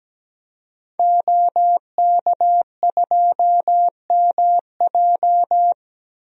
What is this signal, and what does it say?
Morse code, heard as a keyed tone, about 17 wpm, 700 Hz. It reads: OK2MJ
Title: DXCC-100 Call Signs - 5 Characters - 17wpm